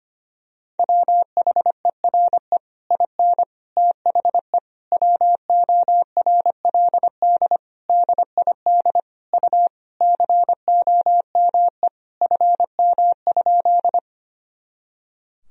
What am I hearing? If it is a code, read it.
Morse code, heard as a keyed tone, about 25 wpm, 700 Hz. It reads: WHERE IN THE WORLD DID U COME FM?
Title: Sentences from Top 200 Words and Abbreviations 25wpm